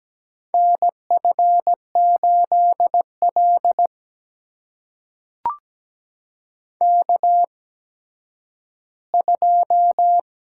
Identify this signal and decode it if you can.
Morse code, heard as a keyed tone, about 17 wpm, spaced as for 10 wpm, 700 Hz. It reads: NF8L K2
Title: USA Call Signs - 8x Character Spacing - 17wpm